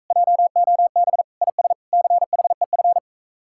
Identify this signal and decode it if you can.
Morse code, heard as a keyed tone, about 45 wpm, 700 Hz. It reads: JOB IS CHEF